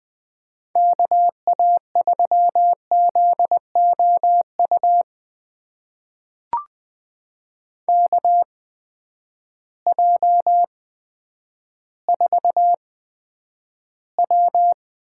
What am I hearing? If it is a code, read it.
Morse code, heard as a keyed tone, about 20 wpm, spaced as for 10 wpm, 700 Hz. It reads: KA3ZOU KJ4W
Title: USA Call Signs - 8x Character Spacing - 20wpm